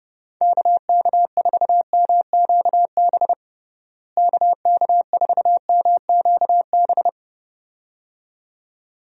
Morse code, heard as a keyed tone, about 30 wpm, 700 Hz. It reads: KK4MQB KK4MQB
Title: USA Call Signs - 6 Characters - Repeated 2x - 30wpm